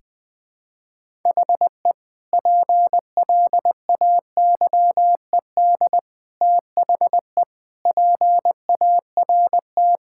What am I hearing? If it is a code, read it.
Morse code, heard as a keyed tone, about 20 wpm, 700 Hz. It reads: HE PLAYED THE PART
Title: Sentences with 4-Word Phrases - 20wpm